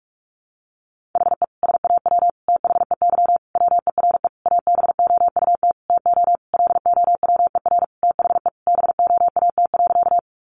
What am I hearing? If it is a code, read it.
Morse code, heard as a keyed tone, about 44 wpm, 700 Hz. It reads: HE SAW THEY WERE ABOUT TO LOWER THE BOAT.
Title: Sentences from Top 700 Words 45wpm